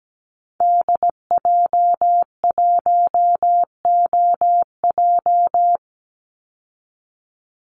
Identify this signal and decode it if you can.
Morse code, heard as a keyed tone, about 17 wpm, 700 Hz. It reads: DJ1OJ
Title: Call Signs Repeated 2x - 17wpm